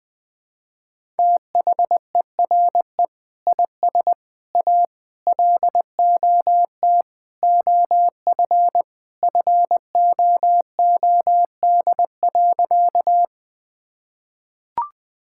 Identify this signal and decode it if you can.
Morse code, heard as a keyed tone, about 20 wpm, 700 Hz. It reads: THERE IS A LOT OF FOOD.
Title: Sentences with 3-Word Phrases - 20wpm